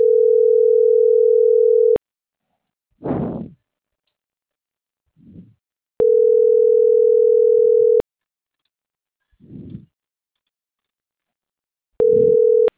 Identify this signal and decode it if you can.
Ringback tone (North America), 440 + 480 Hz.